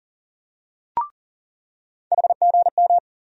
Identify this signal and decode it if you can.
Morse code, heard as a keyed tone, about 40 wpm, 700 Hz. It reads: HGM
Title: USA Call Sign Postfixes - 3 Letters 40wpm